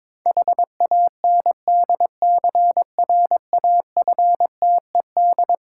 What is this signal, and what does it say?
Morse code, heard as a keyed tone, about 22 wpm, 690 Hz. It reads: HANDCRAFTED
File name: Difficult Words with Hints - 22wpm